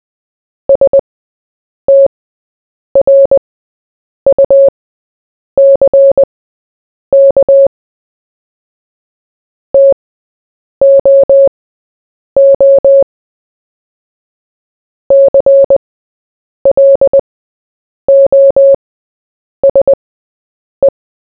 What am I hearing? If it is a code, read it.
Morse code, heard as a keyed tone, about 20 wpm, spaced as for 8 wpm, 550 Hz. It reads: STRUCK TOO CLOSE
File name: Short-Story-4-8wpm